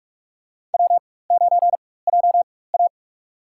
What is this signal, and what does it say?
Morse code, heard as a keyed tone, about 41 wpm, spaced as for 20 wpm, 700 Hz. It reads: W9JA
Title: USA Call Signs - 4 Characters - 4x Character Spacing - 45wpm